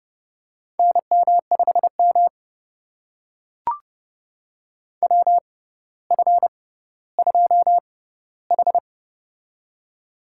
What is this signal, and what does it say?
Morse code, heard as a keyed tone, about 32 wpm, spaced as for 16 wpm, 700 Hz. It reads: NM5M WF2H
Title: USA Call Signs - 4 Characters - 6x Character Spacing - 30wpm